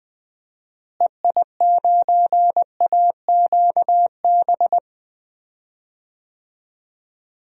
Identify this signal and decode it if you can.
Morse code, heard as a keyed tone, about 20 wpm, 700 Hz. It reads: EI9AQB
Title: DXCC-100 Call Signs - 6 Characters - Repeated 2x - 20wpm